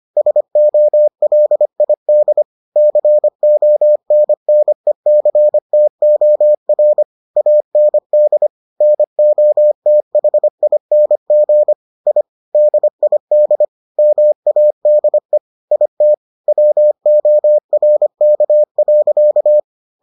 Morse code, heard as a keyed tone, about 25 wpm, 600 Hz. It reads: SOLID CONNECTOR AND NOTHING I DID MADE IT WORK.